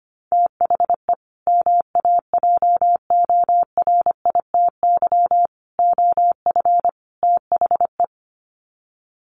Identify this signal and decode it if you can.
Morse code, heard as a keyed tone, about 25 wpm, 700 Hz. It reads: THE MAJORITY OF THE
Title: Sets of 4 Words 25wpm